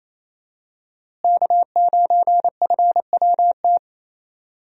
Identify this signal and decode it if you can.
Morse code, heard as a keyed tone, about 28 wpm, 700 Hz. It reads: K9FWT
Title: USA Call Signs - 5 Characters - Repeated 3x - 28wpm